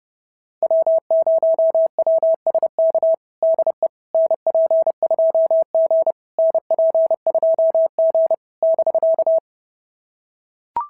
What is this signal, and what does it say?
Morse code, heard as a keyed tone, about 26 wpm, 650 Hz. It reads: W0WSK DE NP2G NP2G <BK>
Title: SOTA QSO Practice 30wpm